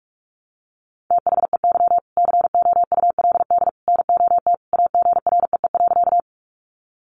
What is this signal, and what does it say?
Morse code, heard as a keyed tone, about 44 wpm, 700 Hz. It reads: THEY COULD NOT AGREE.
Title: Sentences from Top 1100 Words 45wpm